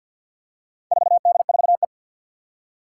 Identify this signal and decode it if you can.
Morse code, heard as a keyed tone, about 52 wpm, 700 Hz. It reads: 4D4E
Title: Four Challenging Letter-Number Combinations 50wpm